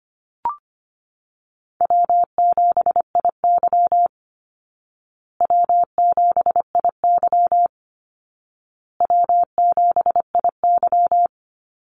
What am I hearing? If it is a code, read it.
Morse code, heard as a keyed tone, about 25 wpm, 700 Hz. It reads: W7IY W7IY W7IY
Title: Call Signs Repeated 2x - 25wpm